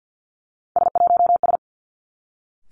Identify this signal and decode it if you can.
Morse code, heard as a keyed tone, about 50 wpm, 700 Hz. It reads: S1S
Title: Three Challenging Letter-Number Combinations 50wpm